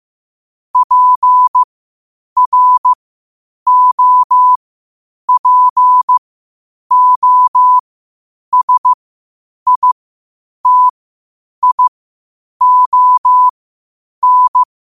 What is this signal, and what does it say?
Morse code, heard as a keyed tone, about 15 wpm, spaced as for 8 wpm, 1000 Hz. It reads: PROPOSITION
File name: qotdfiles / 8wpm